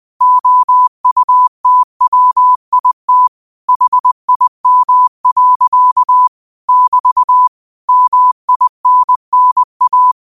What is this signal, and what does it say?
Morse code, heard as a keyed tone, about 20 wpm, 1000 Hz. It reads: OUTWIT HIM. <BT> MINNA